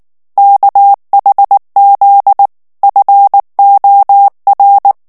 Morse code, heard as a keyed tone, about 19 wpm, 800 Hz. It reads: KHZ FOR
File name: ANGLAIS texte clair 3 vitesse 18 mots minute